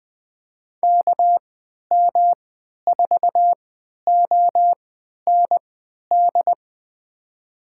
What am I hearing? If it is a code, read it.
Morse code, heard as a keyed tone, about 20 wpm, spaced as for 11 wpm, 700 Hz. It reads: KM4OND
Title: USA Call Signs - 6 Characters - 3x Character Spacing - 20wpm